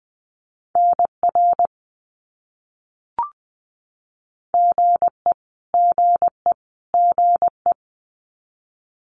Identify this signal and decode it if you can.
Morse code, heard as a keyed tone, about 20 wpm, 700 Hz. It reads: NR GE GE GE
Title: Abbreviations Repeated 3x - 20wpm